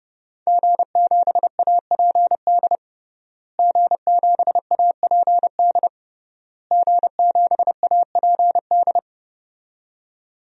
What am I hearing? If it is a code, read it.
Morse code, heard as a keyed tone, about 30 wpm, 700 Hz. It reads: G7APD G7APD G7APD
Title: DXCC-100 Call Signs - 5 Characters - Repeated 3x - 30wpm